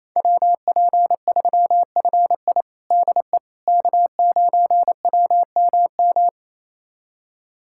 Morse code, heard as a keyed tone, about 28 wpm, 700 Hz. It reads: WP3FI DE K9WMM
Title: USA Callsign DE Callsign - 5 Characters - 28wpm